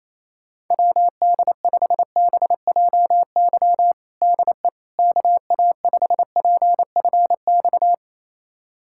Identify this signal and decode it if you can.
Morse code, heard as a keyed tone, about 27 wpm, 700 Hz. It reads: WD5BJY DE KA5PFX
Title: USA Callsign DE Callsign - 6 Characters - 28wpm